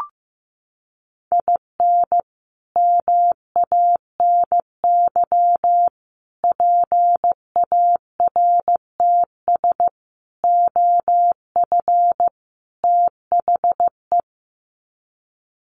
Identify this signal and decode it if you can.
Morse code, heard as a keyed tone, about 15 wpm, 700 Hz. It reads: IN MANY PARTS OF THE
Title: Sets of 5 Words Encore 15wpm